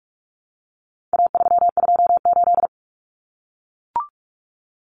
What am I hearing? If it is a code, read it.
Morse code, heard as a keyed tone, about 44 wpm, 700 Hz. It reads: A328
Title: Common Cut Numbers - 4-Digit 45wpm